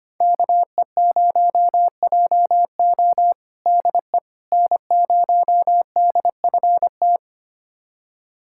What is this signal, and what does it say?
Morse code, heard as a keyed tone, about 25 wpm, 700 Hz. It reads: KE0JO DE N0DFT
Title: USA Callsign DE Callsign - 5 Characters - 25wpm